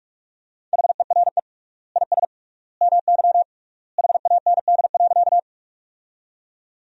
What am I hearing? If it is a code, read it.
Morse code, heard as a keyed tone, about 41 wpm, 700 Hz. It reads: HERE IS MY HAND.
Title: 3x Word-Spacing - Sentences from Top 200 Words - 45wpm